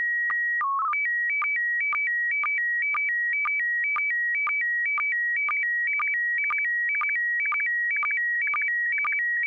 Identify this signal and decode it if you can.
SSTV (slow-scan television), PD 120.
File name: music / example / sstv pd120